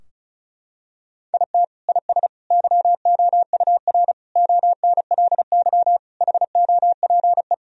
Morse code, heard as a keyed tone, about 34 wpm, 700 Hz. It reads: IT IS YOUR ONLY HOPE
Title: Sentences from Top 1000 Words - Review Part 4 - 35wpm